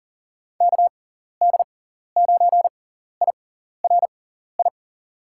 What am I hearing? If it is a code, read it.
Morse code, heard as a keyed tone, about 40 wpm, spaced as for 14 wpm, 700 Hz. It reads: KD9IRI
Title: USA Call Signs - 6x Character Spacing - 40wpm